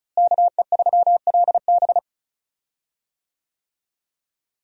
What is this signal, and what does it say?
Morse code, heard as a keyed tone, about 36 wpm, 700 Hz. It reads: KE3LB
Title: USA Call Signs - 5 Characters - Repeated 3x - 35wpm